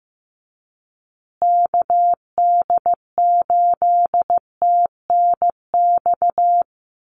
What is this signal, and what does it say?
Morse code, heard as a keyed tone, about 15 wpm, 700 Hz. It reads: KD8TNX
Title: USA Call Signs Repeated 3x 15wpm